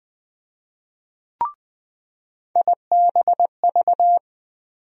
Morse code, heard as a keyed tone, about 20 wpm, 700 Hz. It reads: IBV
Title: USA Call Sign Postfixes - 3 Letters 20wpm